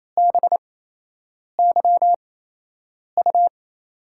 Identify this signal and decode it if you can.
Morse code, heard as a keyed tone, about 28 wpm, spaced as for 8 wpm, 700 Hz. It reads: BYU
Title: USA Call Signs - 8x Character Spacing - 28wpm